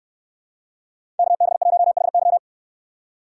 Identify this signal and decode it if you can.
Morse code, heard as a keyed tone, about 68 wpm, 700 Hz. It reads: KB1HY